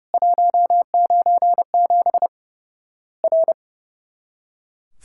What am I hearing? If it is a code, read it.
Morse code, heard as a keyed tone, about 30 wpm, 700 Hz. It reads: 197 R
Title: Worked All Europe DX Contest - QTC Exchange - 30wpm